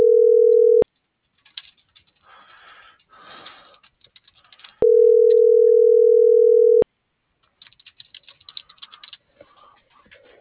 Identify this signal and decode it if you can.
Ringback tone (North America), 440 + 480 Hz.